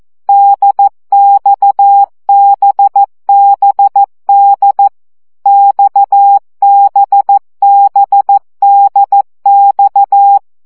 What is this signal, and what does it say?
Morse code, heard as a keyed tone, about 14 wpm, 800 Hz. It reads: DXBBD XBBDX
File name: CW gyakorlas d b x 12 baud